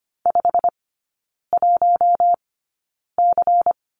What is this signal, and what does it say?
Morse code, heard as a keyed tone, about 25 wpm, spaced as for 9 wpm, 700 Hz. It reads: 51C
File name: Lesson 17e - Letter u - Three Characters 25wpm